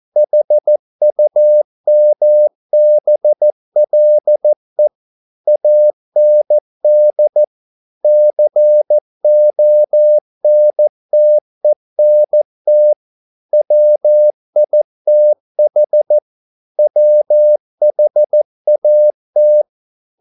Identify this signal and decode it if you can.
Morse code, heard as a keyed tone, about 14 wpm, 600 Hz. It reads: HUMBLE AND CONTENT WITH WHAT
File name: mp3 files / a dogs mistake - 14wpm